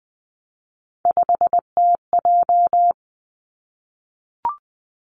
Morse code, heard as a keyed tone, about 20 wpm, 700 Hz. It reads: 5TJ
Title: Three Challenging Letter-Number Combinations 20wpm